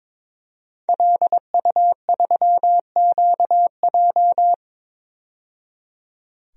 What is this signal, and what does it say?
Morse code, heard as a keyed tone, about 22 wpm, 700 Hz. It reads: LU3QJ
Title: DXCC-100 Call Signs - 5 Characters - 22wpm